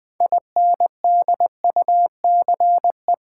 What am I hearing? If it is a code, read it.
Morse code, heard as a keyed tone, about 20 wpm, 700 Hz. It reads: INDUCE
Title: 6 Letter Words - First 3 Letters Given - 20wpm